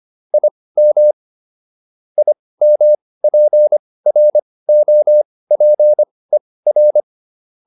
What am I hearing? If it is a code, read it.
Morse code, heard as a keyed tone, about 25 wpm, spaced as for 18 wpm, 600 Hz. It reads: IM IMPROPER